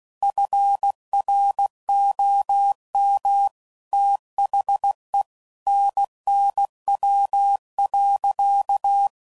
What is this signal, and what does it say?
Morse code, heard as a keyed tone, about 16 wpm, 800 Hz. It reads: FROM THE NNW.
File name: CW Code Groups @15WPM / Weather Sets / Weather 5 22